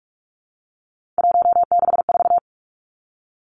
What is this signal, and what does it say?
Morse code, heard as a keyed tone, about 46 wpm, 700 Hz. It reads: 164
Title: Three Challenging Letter-Number Combinations 45wpm